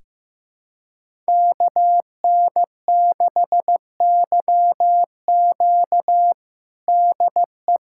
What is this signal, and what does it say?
Morse code, heard as a keyed tone, about 15 wpm, 700 Hz. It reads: KN6YQ DE
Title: USA Callsign DE Callsign - 5 Characters - 15wpm